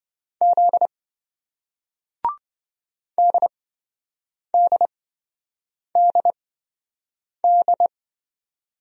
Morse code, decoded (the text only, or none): Z DDDD